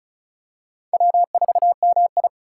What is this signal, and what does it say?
Morse code, heard as a keyed tone, about 36 wpm, 700 Hz. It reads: W4MI
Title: USA Call Signs - 3x Character Spacing - 35wpm